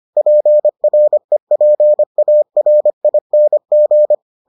Morse code, heard as a keyed tone, about 25 wpm, 600 Hz. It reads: PREPARING